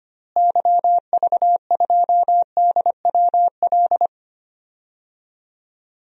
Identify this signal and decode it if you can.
Morse code, heard as a keyed tone, about 25 wpm, 700 Hz. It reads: YV2DWL
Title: DXCC-100 Call Signs - 6 Characters - Repeated 3x - 25wpm